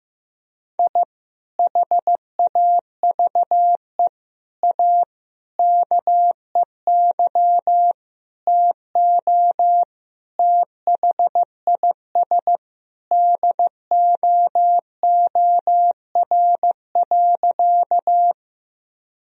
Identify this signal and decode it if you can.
Morse code, heard as a keyed tone, about 16 wpm, 700 Hz. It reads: I HAVE A KEY TO THIS DOOR.
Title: Sentences from Top 1000 Words - Review Part 4 - 15wpm